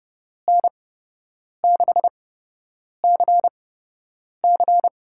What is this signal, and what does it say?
Morse code, heard as a keyed tone, about 31 wpm, spaced as for 8 wpm, 700 Hz. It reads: N6CC
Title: USA Call Signs - 5 Characters - 8x Character Spacing - 30wpm